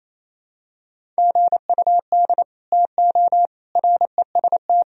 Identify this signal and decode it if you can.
Morse code, heard as a keyed tone, about 28 wpm, 700 Hz. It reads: GUD TO REST
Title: Sentences from Top 500 Words and Abbreviations 28wpm